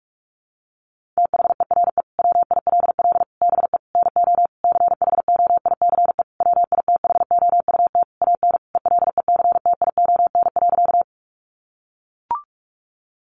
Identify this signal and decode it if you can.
Morse code, heard as a keyed tone, about 44 wpm, 700 Hz. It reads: THERE WILL BE NO CHOICE WITHOUT AN ELECTION.
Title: Sentences from Top 1500 Words 45wpm